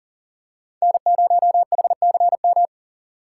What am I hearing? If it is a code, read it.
Morse code, heard as a keyed tone, about 40 wpm, 700 Hz. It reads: N0HCM
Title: USA Call Signs - 5 Characters - 3x Character Spacing - 40wpm